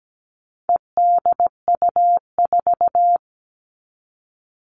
Morse code, heard as a keyed tone, about 17 wpm, 700 Hz. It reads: EDU4
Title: All Cut Numbers - 4-Digit 17wpm